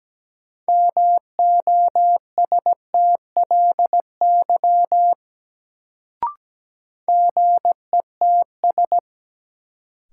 Morse code, heard as a keyed tone, about 17 wpm, 700 Hz. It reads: MOSTLY GETS
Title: Sentences from Top 1300 Words - New Word Drill Alt - 17wpm